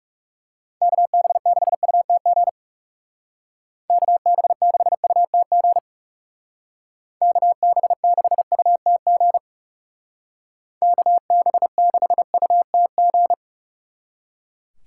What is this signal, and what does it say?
Morse code, heard as a keyed tone, about 36 wpm, 700 Hz. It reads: KB6UTG KB6UTG KB6UTG KB6UTG
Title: Speed Racing - USA 2x3 Call Signs 30wpm